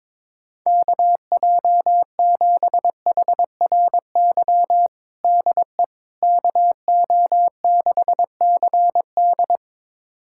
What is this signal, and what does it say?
Morse code, heard as a keyed tone, about 22 wpm, 700 Hz. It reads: KJ7HRY DE KO6CD
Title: USA Callsign DE Callsign 22wpm